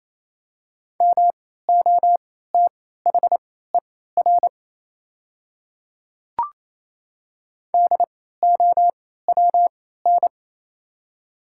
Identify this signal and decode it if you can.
Morse code, heard as a keyed tone, about 28 wpm, spaced as for 16 wpm, 700 Hz. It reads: MOTHER DOWN
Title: 3x Character Spacing - Top 200 Words - 28wpm